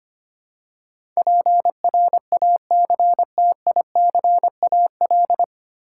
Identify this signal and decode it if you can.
Morse code, heard as a keyed tone, about 25 wpm, 700 Hz. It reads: PRACTICAL
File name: practice / 25wpm / 9 Letter Words - First 3 Letters Given - 25wpm